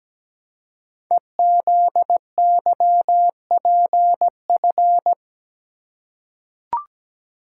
Morse code, heard as a keyed tone, about 17 wpm, 700 Hz. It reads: EZYPF
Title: Code Groups - 5 Letters - 17wpm